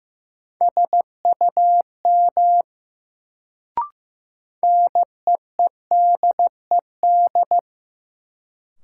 Morse code, heard as a keyed tone, about 15 wpm, 700 Hz. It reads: SUM NEEDED
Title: Sentences from Top 1000 Words - Review Part 6 - New Word Drill - 15wpm